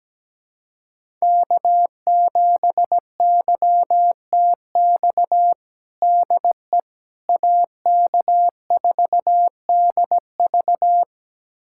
Morse code, heard as a keyed tone, about 17 wpm, 700 Hz. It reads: K7YTX DE AK4DV
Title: USA Callsign DE Callsign - 5 Characters - 17wpm